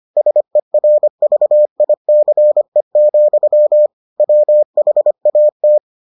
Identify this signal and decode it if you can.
Morse code, heard as a keyed tone, about 25 wpm, 600 Hz. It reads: SERVICE, WHAT